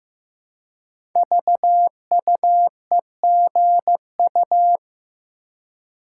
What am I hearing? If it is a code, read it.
Morse code, heard as a keyed tone, about 15 wpm, 700 Hz. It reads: VUEGU
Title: Code Groups - 5 Letters - 15wpm